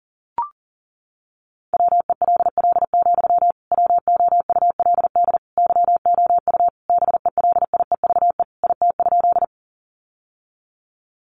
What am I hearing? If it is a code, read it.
Morse code, heard as a keyed tone, about 40 wpm, 700 Hz. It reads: WELL, WOULD YOU BELIEVE IT?
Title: Sentences from Top 1000 Words 40wpm